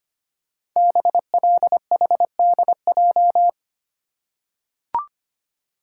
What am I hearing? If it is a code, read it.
Morse code, heard as a keyed tone, about 25 wpm, 700 Hz. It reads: BLHDJ
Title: Code Groups - 5 Letters - 25wpm